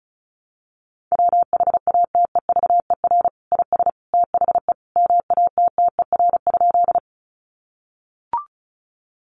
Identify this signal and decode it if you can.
Morse code, heard as a keyed tone, about 34 wpm, 700 Hz. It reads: WHATEVER IS THE MATTER?
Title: Sentences from Top 1100 Words 35wpm